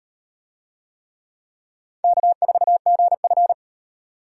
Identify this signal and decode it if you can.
Morse code, heard as a keyed tone, about 36 wpm, 700 Hz. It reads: K4GF